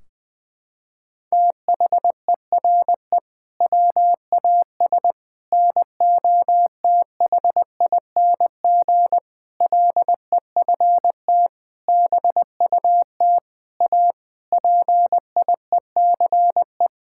Morse code, heard as a keyed tone, about 20 wpm, 700 Hz. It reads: THERE WAS NOTHING LEFT BUT A PIECE
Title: Sentences from Top 1600 Words 20wpm